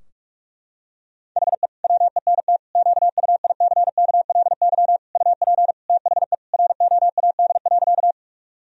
Morse code, heard as a keyed tone, about 46 wpm, 700 Hz. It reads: HE WENT QUICKLY UP THE ROAD.
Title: Sentences from Top 1000 Words - Review Part 2 - 45wpm